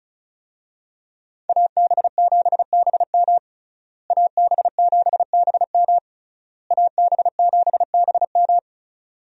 Morse code, heard as a keyed tone, about 36 wpm, 700 Hz. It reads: AB7BM AB7BM AB7BM
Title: USA Call Signs - 5 Characters - Repeated 3x - 35wpm